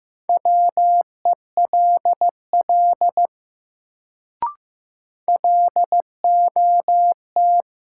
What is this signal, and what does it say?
Morse code, heard as a keyed tone, about 15 wpm, 700 Hz. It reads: WELL LOT
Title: Sets of 2 Words - New Word Drill - 15wpm